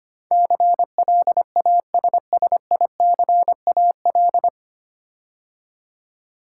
Morse code, heard as a keyed tone, about 25 wpm, 700 Hz. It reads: CLASSICAL
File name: practice / 25wpm / 9 Letter Words - First 3 Letters Given - 25wpm